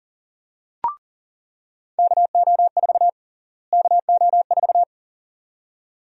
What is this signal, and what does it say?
Morse code, heard as a keyed tone, about 40 wpm, 700 Hz. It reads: KO4 KO4
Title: USA Call Sign Prefixes - Repeated 2x - 40wpm